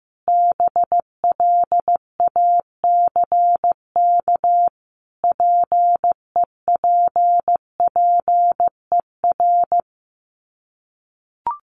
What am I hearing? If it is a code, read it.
Morse code, heard as a keyed tone, about 15 wpm, 700 Hz. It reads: BLACK PEPPER
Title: Sets of 5 Words Encore 15wpm